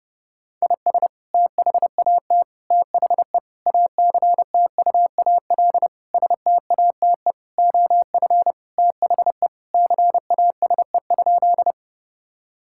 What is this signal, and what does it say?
Morse code, heard as a keyed tone, about 30 wpm, 700 Hz. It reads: IS THAT THE ACTUAL STATE OF THE CASE?
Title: Sentences from Top 1000 Words - Review Part 5 - 30wpm